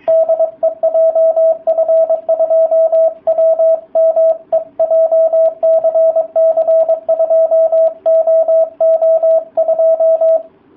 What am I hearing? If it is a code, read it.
Morse code, heard as a keyed tone, about 22 wpm, 660 Hz, with radio noise. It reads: DE JF2WME JCC2OO2